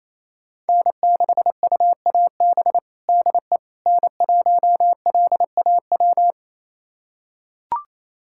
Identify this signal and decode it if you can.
Morse code, heard as a keyed tone, about 28 wpm, 700 Hz. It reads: N6UAB DE N1LAW
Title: USA Callsign DE Callsign - 5 Characters - 28wpm